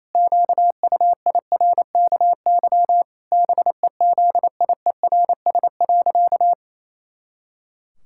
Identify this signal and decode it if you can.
Morse code, heard as a keyed tone, about 27 wpm, 700 Hz. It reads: QUIRKY BEZIERS.
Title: Pangram Challenge v2 - 28wpm